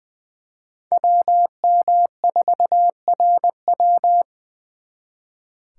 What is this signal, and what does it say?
Morse code, heard as a keyed tone, about 23 wpm, 700 Hz. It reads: WM4RW
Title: Speed Racing - USA 2x2 Call Signs 20wpm